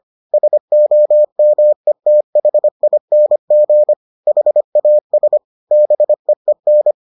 Morse code, heard as a keyed tone, about 25 wpm, 600 Hz. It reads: SOMETHING HAS BEEN